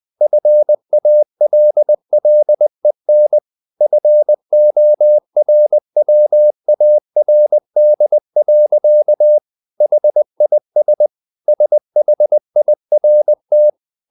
Morse code, heard as a keyed tone, about 20 wpm, 600 Hz. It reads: FALLEN FORWARD. HIS SHIRT
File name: the old man and the sea / the old man and the sea part 1 20wpm